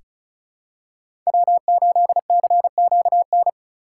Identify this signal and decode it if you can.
Morse code, heard as a keyed tone, about 34 wpm, 700 Hz. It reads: W8CQN
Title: USA Call Signs - 5 Characters - 6x Character Spacing - 35wpm